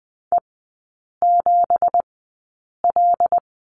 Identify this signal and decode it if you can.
Morse code, heard as a keyed tone, about 20 wpm, spaced as for 8 wpm, 700 Hz. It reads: E7L
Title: 184 Lesson 30e - Letter b - Three Characters 20wpm